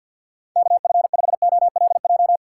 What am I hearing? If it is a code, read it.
Morse code, heard as a keyed tone, about 53 wpm, 700 Hz. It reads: KF5OLJ
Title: USA Call Signs - 6 Characters - 6x Character Spacing - 50wpm